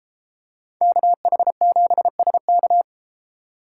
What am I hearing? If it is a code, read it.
Morse code, heard as a keyed tone, about 32 wpm, 700 Hz. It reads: KH7SK